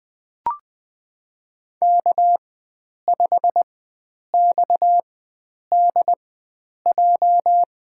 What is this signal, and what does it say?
Morse code, heard as a keyed tone, about 21 wpm, spaced as for 9 wpm, 700 Hz. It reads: K5XDJ
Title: USA Call Signs - 4x Character Spacing - 20wpm